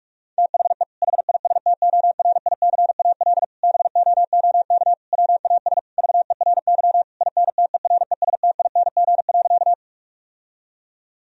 Morse code, heard as a keyed tone, about 46 wpm, 700 Hz. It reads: THE HISTORICAL BOOK WAS VERY INTERESTING.
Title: Sentences from Top 1000 Words - Review Part 4 - 45wpm